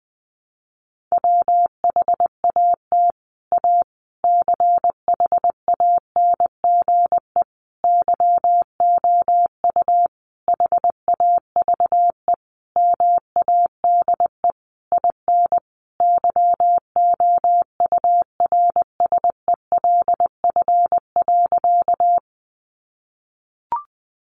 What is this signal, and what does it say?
Morse code, heard as a keyed tone, about 20 wpm, 700 Hz. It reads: WHAT A CHANGE YOU HAVE MADE IN YOURSELF.
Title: Sentences from Top 600 Words 20wpm